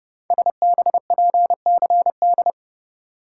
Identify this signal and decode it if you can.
Morse code, heard as a keyed tone, about 29 wpm, 690 Hz. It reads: SBPCD
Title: Code Groups - 5 Letters - 30wpm